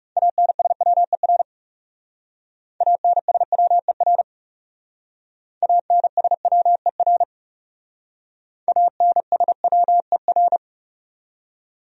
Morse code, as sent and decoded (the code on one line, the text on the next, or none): .- -. ... .-- . .-. / .- -. ... .-- . .-. / .- -. ... .-- . .-. / .- -. ... .-- . .-.
ANSWER ANSWER ANSWER ANSWER